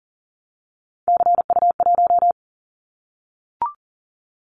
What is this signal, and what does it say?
Morse code, heard as a keyed tone, about 40 wpm, 700 Hz. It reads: CU1
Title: Callsign Prefixes for Top 100 DXCC - 40wpm